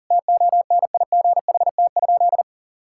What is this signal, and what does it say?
Morse code, heard as a keyed tone, about 40 wpm, 700 Hz. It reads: TONIGHT?
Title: Sentences from Top 1000 Words - Review Part 5 - 40wpm